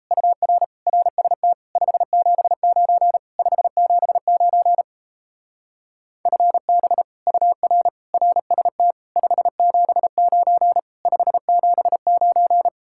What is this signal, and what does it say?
Morse code, heard as a keyed tone, about 34 wpm, 700 Hz. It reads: UR RST 579 579 FB UR RST 579 579
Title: Speed-Racing Common QSO Phrases - 28wpm